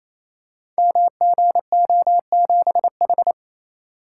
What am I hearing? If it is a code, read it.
Morse code, heard as a keyed tone, about 28 wpm, 700 Hz. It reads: MGO7H
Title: Code Groups - 5 Letter:Numbers - 28wpm